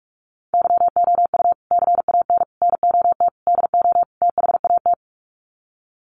Morse code, heard as a keyed tone, about 44 wpm, 700 Hz. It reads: YOU CAN NOT DO THAT
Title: Sets of 5 Words 45wpm